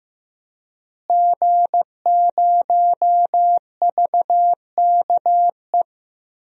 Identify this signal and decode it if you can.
Morse code, heard as a keyed tone, about 15 wpm, 700 Hz. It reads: G0VKE
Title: DXCC-100 Call Signs - 5 Characters - Repeated 3x - 15wpm